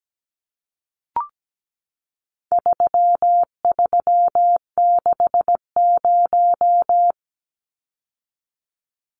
Morse code, heard as a keyed tone, about 17 wpm, 700 Hz. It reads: 3360
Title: Numbers - 4-Digit 17wpm